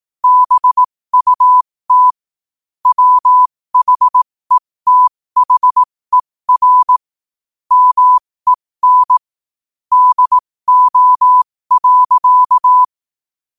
Morse code, heard as a keyed tone, about 18 wpm, spaced as for 15 wpm, 1000 Hz. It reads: BUT WHETHER MEN DO.
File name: qotdfiles / 15wpm